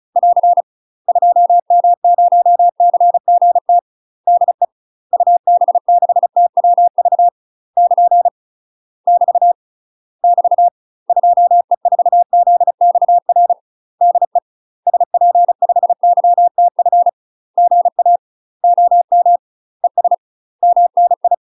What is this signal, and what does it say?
Morse code, heard as a keyed tone, about 34 wpm, 700 Hz. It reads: <AR> 2M0CGT DE UB6TWV <KN> <BT> <BT> 2E4ZXR DE SP5YTF GA OM ES MNI
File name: UK MASTER QSOs TRAINING / 35WPM / UK MASTER QSOs 0042